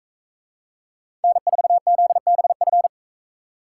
Morse code, heard as a keyed tone, about 42 wpm, 700 Hz. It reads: N4ZBF